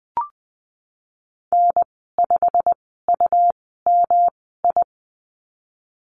Morse code, heard as a keyed tone, about 20 wpm, spaced as for 15 wpm, 700 Hz. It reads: N5UMI
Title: 154 Lesson 25g - Letter p - Call Signs 20wpm